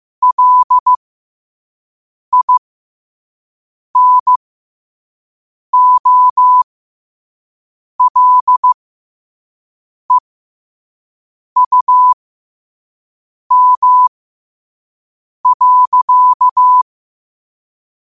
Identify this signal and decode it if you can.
Morse code, heard as a keyed tone, about 15 wpm, spaced as for 5 wpm, 1000 Hz. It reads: LINOLEUM.
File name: qotdfiles / 5wpm